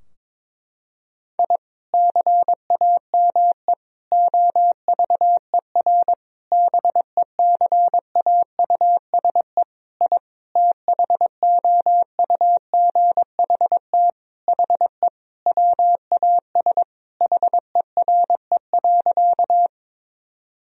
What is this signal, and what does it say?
Morse code, heard as a keyed tone, about 22 wpm, 700 Hz. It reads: I CAME OVER BECAUSE I THOUGHT HE WAS HERE.
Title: Sentences from Top 1000 Words - Review Part 3 - 22wpm